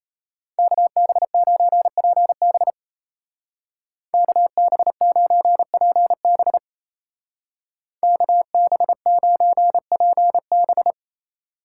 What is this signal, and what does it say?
Morse code, heard as a keyed tone, about 34 wpm, 700 Hz. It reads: KB9PB KB9PB KB9PB
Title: Speed Racing - USA 2x2 Call Signs 28wpm